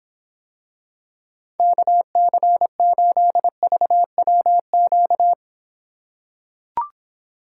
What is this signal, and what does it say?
Morse code, heard as a keyed tone, about 26 wpm, 700 Hz. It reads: KC8VWQ